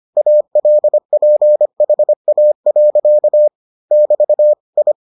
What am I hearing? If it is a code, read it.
Morse code, heard as a keyed tone, about 25 wpm, 600 Hz. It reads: ALPHA. <BT> I